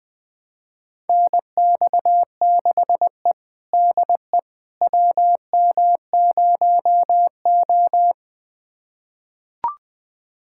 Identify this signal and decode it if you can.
Morse code, heard as a keyed tone, about 21 wpm, 700 Hz. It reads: NX6E DE WM0O
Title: USA Callsign DE Callsign - 4 Characters - 20wpm